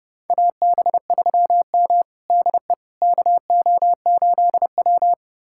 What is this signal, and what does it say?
Morse code, heard as a keyed tone, about 30 wpm, 700 Hz. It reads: AB3M DE KO8W
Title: USA Callsign DE Callsign - 4 Characters - 30wpm